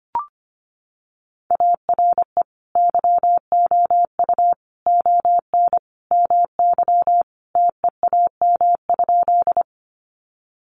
Morse code, heard as a keyed tone, about 25 wpm, 700 Hz. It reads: ARE YOU ON MY TEAM?
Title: Sentences from Top 1500 Words 25wpm